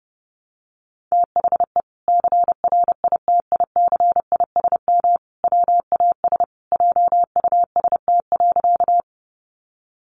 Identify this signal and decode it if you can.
Morse code, heard as a keyed tone, about 30 wpm, 700 Hz. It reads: THE CRITICISM WAS JUST.
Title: Sentences from Top 1300 Words 30wpm